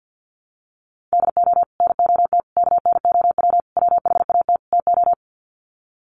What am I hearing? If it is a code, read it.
Morse code, heard as a keyed tone, about 48 wpm, 700 Hz. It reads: DO NOT KNOW WHAT TO
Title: Sets of 5 Words 50wpm